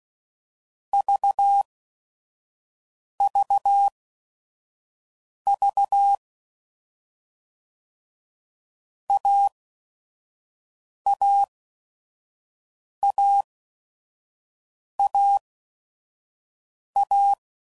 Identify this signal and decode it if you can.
Morse code, heard as a keyed tone, about 16 wpm, spaced as for 5 wpm, 800 Hz. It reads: VVV AAAAA
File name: Basic Letter Groups By Blocks / Groups@5 / AWUNDG@5 f15